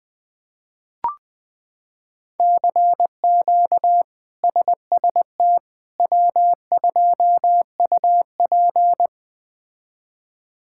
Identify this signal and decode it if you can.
Morse code, heard as a keyed tone, about 20 wpm, 700 Hz. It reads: CQ SST W2UP
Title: K1USN SST QSO Practice - 20wpm